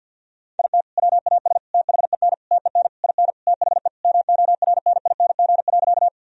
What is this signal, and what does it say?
Morse code, heard as a keyed tone, about 50 wpm, 700 Hz. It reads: IT WAS THEN TEN IN THE MORNING.
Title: Sentences from Top 1000 Words - Review Part 2 - 50wpm